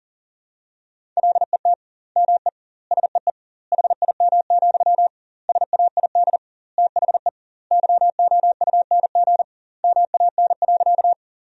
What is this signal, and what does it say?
Morse code, heard as a keyed tone, about 40 wpm, 700 Hz. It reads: LET ME SEE HIM, SAID THE YOUNG MAN.
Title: 2x Word-Spacing - Sentences from Top 400 Words - 40wpm